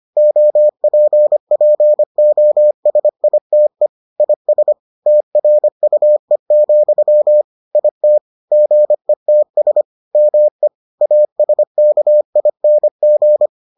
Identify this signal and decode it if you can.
Morse code, heard as a keyed tone, about 25 wpm, 600 Hz. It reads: OPPOSITE IS TRUE, IT GETS ME ASKING